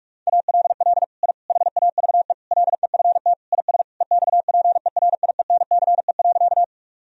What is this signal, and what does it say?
Morse code, heard as a keyed tone, about 45 wpm, 700 Hz. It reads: ALL I HAVE LEFT IS EXPERIENCE.
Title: Sentences from Top 1000 Words - Review Part 3 - 45wpm